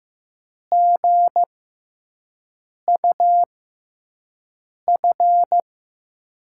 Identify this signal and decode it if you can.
Morse code, heard as a keyed tone, about 15 wpm, spaced as for 5 wpm, 700 Hz. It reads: GUF
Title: USA Call Signs - 6 Characters - 6x Character Spacing - 15wpm